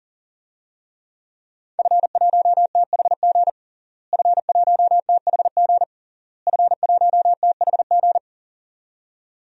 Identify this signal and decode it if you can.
Morse code, heard as a keyed tone, about 40 wpm, 700 Hz. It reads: F1THG F1THG F1THG
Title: DXCC-100 Call Signs - 5 Characters - Repeated 3x - 40wpm